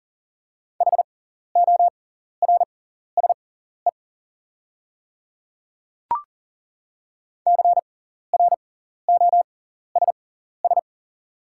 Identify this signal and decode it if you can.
Morse code, heard as a keyed tone, about 40 wpm, spaced as for 14 wpm, 700 Hz. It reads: HORSE CROSS
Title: 6x Character Spacing - Top 300 Words - 40wpm